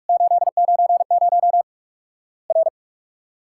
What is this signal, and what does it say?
Morse code, heard as a keyed tone, about 43 wpm, 700 Hz. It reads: 890 R
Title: Worked All Europe DX Contest - QTC Exchange - 45wpm